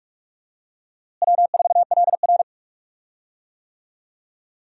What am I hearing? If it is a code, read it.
Morse code, heard as a keyed tone, about 46 wpm, 700 Hz. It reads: W4LR
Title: USA Call Signs - 4 Characters - Repeated 2x - 45wpm